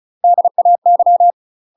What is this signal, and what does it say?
Morse code, heard as a keyed tone, about 36 wpm, 700 Hz. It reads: DAY
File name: UK MASTER QSOs TRAINING / 35WPM / UK MASTER QSOs 0010